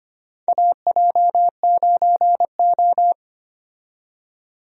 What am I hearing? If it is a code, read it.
Morse code, heard as a keyed tone, about 25 wpm, 700 Hz. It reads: AJ9O
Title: USA Call Signs - 4 Characters - 25wpm